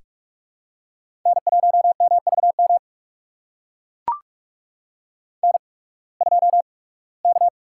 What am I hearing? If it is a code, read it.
Morse code, heard as a keyed tone, about 44 wpm, spaced as for 23 wpm, 690 Hz. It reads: N1MVM N2K
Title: USA Call Signs - 5 Characters - 8x Character Spacing - 45wpm